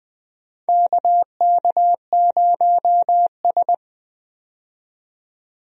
Morse code, heard as a keyed tone, about 20 wpm, 700 Hz. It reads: KK0S
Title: USA Call Signs - 4 Characters - 20wpm